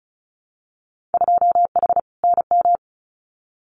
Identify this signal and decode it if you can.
Morse code, heard as a keyed tone, about 36 wpm, 700 Hz. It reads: 2H NM
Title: Winter Field Day Exchanges Encore 35wpm